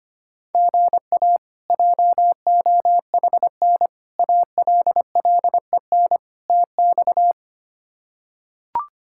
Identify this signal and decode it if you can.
Morse code, heard as a keyed tone, about 25 wpm, 700 Hz. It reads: GA JOHN ALLEN TX
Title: K1USN SST QSO Practice - 25wpm